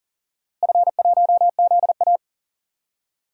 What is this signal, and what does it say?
Morse code, heard as a keyed tone, about 40 wpm, 700 Hz. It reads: F1ZA
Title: DXCC-100 Call Signs - 4 Characters - Repeated 3x - 40wpm